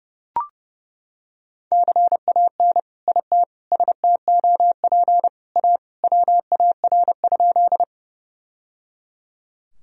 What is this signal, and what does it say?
Morse code, heard as a keyed tone, about 30 wpm, 700 Hz. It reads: CAN IT STOP A WAR?
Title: Sentences from Top 1000 Words - Review Part 2 - 30wpm